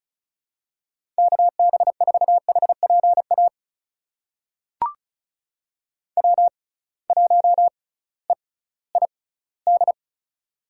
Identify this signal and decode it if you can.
Morse code, heard as a keyed tone, about 36 wpm, spaced as for 19 wpm, 700 Hz. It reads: KB4HPA W1EID
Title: USA Call Signs - 6x Character Spacing - 35wpm